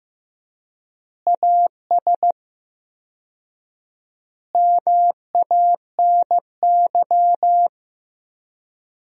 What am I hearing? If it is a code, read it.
Morse code, heard as a keyed tone, about 15 wpm, 700 Hz. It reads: AS MANY
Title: 4x Word-Spacing - Sentences from Top 400 Words - 15wpm